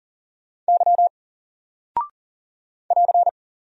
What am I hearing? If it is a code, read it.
Morse code, heard as a keyed tone, about 40 wpm, 700 Hz. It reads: Y <AR>
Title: Single Character (Letter:Number:Punctuation:Prosign) 40wpm